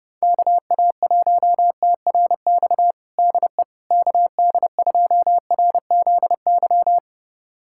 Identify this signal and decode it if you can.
Morse code, heard as a keyed tone, about 30 wpm, 700 Hz. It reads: KA1TRX DE KD2RZY
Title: USA Callsign DE Callsign - 6 Characters - 30wpm